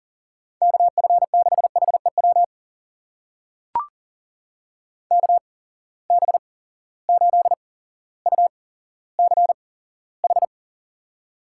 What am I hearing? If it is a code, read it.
Morse code, heard as a keyed tone, about 40 wpm, spaced as for 17 wpm, 700 Hz. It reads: KF6HEW KB8UCH
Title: USA Call Signs - 6 Characters - 8x Character Spacing - 40wpm